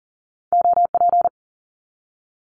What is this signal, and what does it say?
Morse code, heard as a keyed tone, about 42 wpm, 700 Hz. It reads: OP
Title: Abbreviations 40wpm